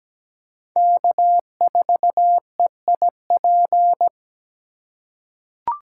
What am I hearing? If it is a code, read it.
Morse code, heard as a keyed tone, about 17 wpm, 700 Hz. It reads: K4EIP
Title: USA Call Signs - 4x Character Spacing - 17wpm